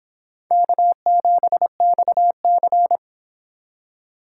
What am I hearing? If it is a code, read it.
Morse code, heard as a keyed tone, about 31 wpm, 700 Hz. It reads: K7XC